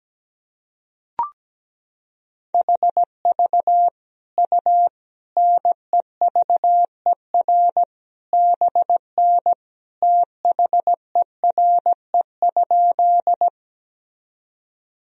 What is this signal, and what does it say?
Morse code, heard as a keyed tone, about 17 wpm, 700 Hz. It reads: HV U NEVER BN THERE?
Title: Sentences from Top 300 Words and Abbreviations 17wpm